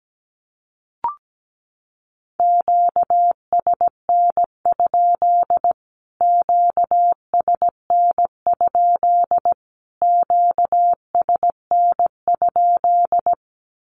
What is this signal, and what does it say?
Morse code, heard as a keyed tone, about 17 wpm, 700 Hz. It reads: QSN? QSN? QSN?
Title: Q-Codes Repeated 3x 17wpm